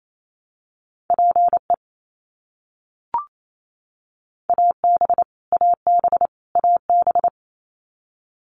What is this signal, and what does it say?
Morse code, heard as a keyed tone, about 28 wpm, 700 Hz. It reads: PE AB AB AB
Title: Canadian Provinces and Territories - Repeated 3x - 28wpm